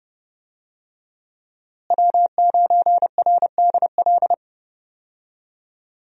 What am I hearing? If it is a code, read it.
Morse code, heard as a keyed tone, about 32 wpm, 700 Hz. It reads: W9RDL
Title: USA Call Signs - 5 Characters - Repeated 2x - 30wpm